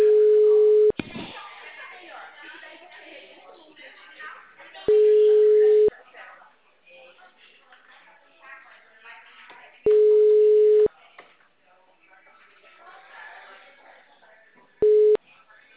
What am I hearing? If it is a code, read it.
Ringback tone (Europe, 425 Hz), with speech.
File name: out- -2008- - - .168